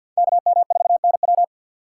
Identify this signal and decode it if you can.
Morse code, heard as a keyed tone, about 53 wpm, 700 Hz. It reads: KM4NW
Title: USA Call Signs - 5 Characters - 4x Character Spacing - 50wpm